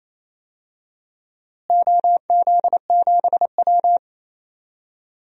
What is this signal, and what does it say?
Morse code, heard as a keyed tone, about 28 wpm, 700 Hz. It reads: OZ7W
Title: DXCC-100 Call Signs - 4 Characters - Repeated 2x - 28wpm